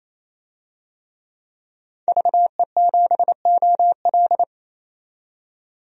Morse code, heard as a keyed tone, about 27 wpm, 700 Hz. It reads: VE7OL
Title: DXCC-100 Call Signs - 5 Characters - 28wpm